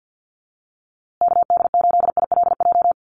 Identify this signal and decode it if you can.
Morse code, heard as a keyed tone, about 50 wpm, 700 Hz. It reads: KD8ILJ
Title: USA Call Signs Repeated 3x 50wpm